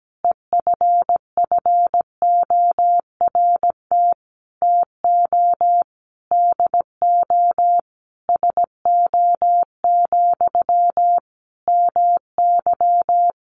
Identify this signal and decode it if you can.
Morse code, heard as a keyed tone, about 17 wpm, 700 Hz. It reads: EFFORT TO DO SO, MY
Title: Sentences from Top 1200 Words 17wpm